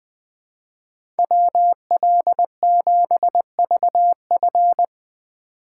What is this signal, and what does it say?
Morse code, heard as a keyed tone, about 21 wpm, 700 Hz. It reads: WL7VF
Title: USA Call Signs - 5 Characters - 2x Character Spacing - 20wpm